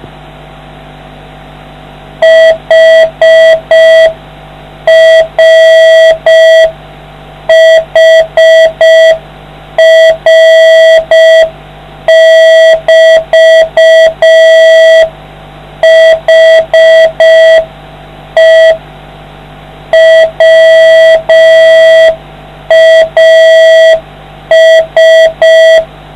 Morse code, heard as a keyed tone, about 5 wpm, 650 Hz, with noise. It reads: HR HR <BT> HE WAS